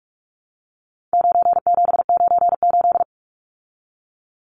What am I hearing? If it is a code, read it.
Morse code, heard as a keyed tone, about 44 wpm, 700 Hz. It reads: 9798
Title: Numbers - 4-Digit 45wpm